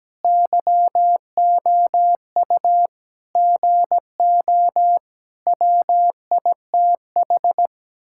Morse code, heard as a keyed tone, about 17 wpm, 700 Hz. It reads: YOU GO WITH
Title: Sentences from Top 1000 Words - Review Part 6 - 17wpm